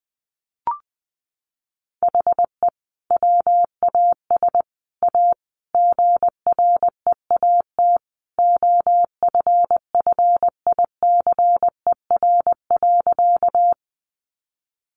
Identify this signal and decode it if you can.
Morse code, heard as a keyed tone, about 19 wpm, 700 Hz. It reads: HE WAS A GREAT OFFICER.
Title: Sentences from Top 1200 Words 20wpm